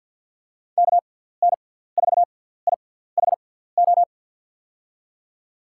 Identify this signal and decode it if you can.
Morse code, heard as a keyed tone, about 47 wpm, spaced as for 17 wpm, 700 Hz. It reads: KN4IHO
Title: USA Call Signs - 6 Characters - 6x Character Spacing - 50wpm